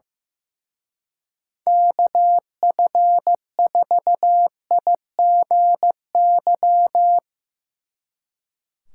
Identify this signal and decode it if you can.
Morse code, heard as a keyed tone, about 16 wpm, 700 Hz. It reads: KF4IGY
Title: Speed Racing - USA 2x3 Call Signs 15wpm